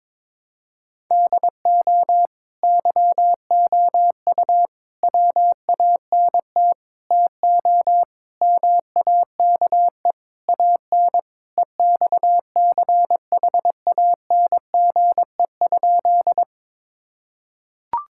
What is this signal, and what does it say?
Morse code, heard as a keyed tone, about 22 wpm, 700 Hz. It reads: DO YOU WANT TO MAKE AN EXCHANGE?
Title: Sentences from Top 1000 Words - Review Part 6 - 22wpm